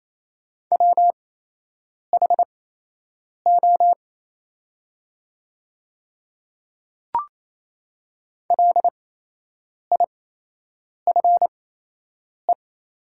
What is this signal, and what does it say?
Morse code, heard as a keyed tone, about 28 wpm, spaced as for 8 wpm, 700 Hz. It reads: WHO LIFE
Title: 8x Character Spacing - Top 500 Words - 28wpm